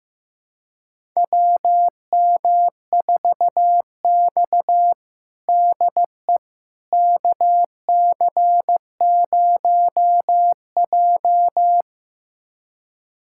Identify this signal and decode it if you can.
Morse code, heard as a keyed tone, about 15 wpm, 700 Hz. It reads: WM4X DE KC0J
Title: USA Callsign DE Callsign - 4 Characters - 15wpm